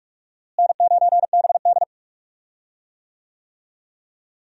Morse code, heard as a keyed tone, about 43 wpm, 700 Hz. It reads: N9BD
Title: USA Call Signs - 4 Characters - Repeated 2x - 45wpm